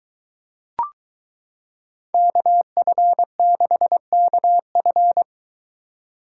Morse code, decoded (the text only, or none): KF6KF